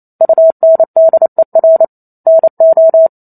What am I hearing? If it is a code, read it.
Morse code, heard as a keyed tone, about 28 wpm, 650 Hz. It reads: UNDER NO
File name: Chapter 2 - 28 WPM